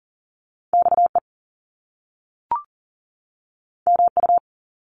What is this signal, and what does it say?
Morse code, heard as a keyed tone, about 39 wpm, 700 Hz. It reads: XE MU
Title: Top Two Letter Combinations in Words 40wpm